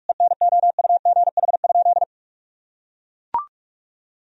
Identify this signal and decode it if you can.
Morse code, heard as a keyed tone, about 43 wpm, 700 Hz. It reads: ENOUGH?
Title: Sentences from Top 1000 Words - Review Part 2 - 45wpm